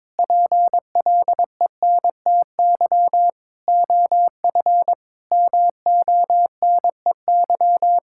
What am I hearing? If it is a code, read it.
Morse code, heard as a keyed tone, about 22 wpm, 700 Hz. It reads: PLENTY OF MONEY
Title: Sentences from Top 1600 Words 22wpm